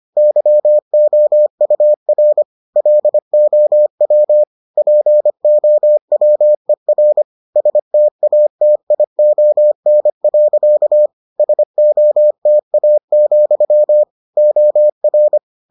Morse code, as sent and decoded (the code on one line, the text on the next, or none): -.-- --- ..- .-. / .-.. --- .-- / .--. --- .-- . .-. / ... - .- - .. --- -. .-.-.- / ... --- - .- --..-- / --- .-.
YOUR LOW POWER STATION. SOTA, OR